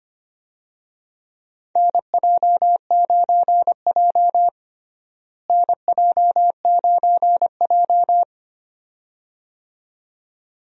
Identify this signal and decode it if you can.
Morse code, heard as a keyed tone, about 25 wpm, 700 Hz. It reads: NJ9J NJ9J
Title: USA Call Signs - 4 Characters - Repeated 2x - 25wpm